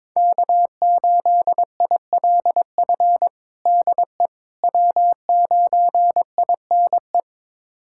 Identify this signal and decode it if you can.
Morse code, heard as a keyed tone, about 21 wpm, 700 Hz. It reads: K8ILF DE W9INE
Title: USA Callsign DE Callsign - 5 Characters - 22wpm